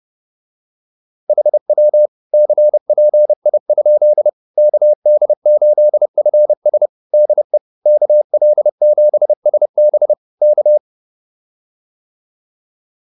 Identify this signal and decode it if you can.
Morse code, heard as a keyed tone, about 30 wpm, 600 Hz. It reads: HW CPI? KD8FS DE KL7SB K